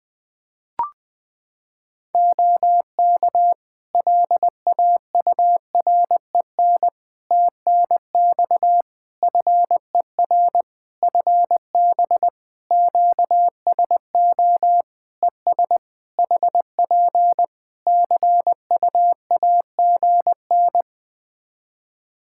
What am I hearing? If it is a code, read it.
Morse code, heard as a keyed tone, about 20 wpm, 700 Hz. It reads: OK LAUREN TNX FER FB QSO ES HP CUAGN
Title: Practice QSO - Exchange 4 - Ending - 20wpm